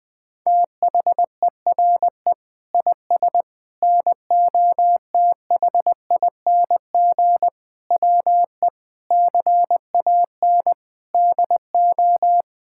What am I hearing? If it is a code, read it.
Morse code, heard as a keyed tone, about 20 wpm, 700 Hz. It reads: THERE IS NOTHING WE CAN DO
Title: Sentences from Top 1000 Words - Review Part 6 - 20wpm